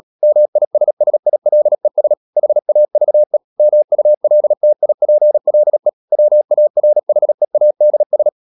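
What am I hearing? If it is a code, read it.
Morse code, heard as a keyed tone, about 37 wpm, 600 Hz. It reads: MISSILES HAVE MULTIPLE WARHEADS
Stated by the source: Morse code practice at 35–39 wpm sending English text of Ken Follett's novel Never, from CW med Gnister.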